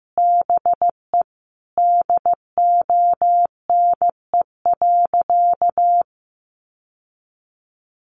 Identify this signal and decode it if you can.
Morse code, heard as a keyed tone, about 15 wpm, 700 Hz. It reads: BE DONE.
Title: Sentences from Top 700 Words 15wpm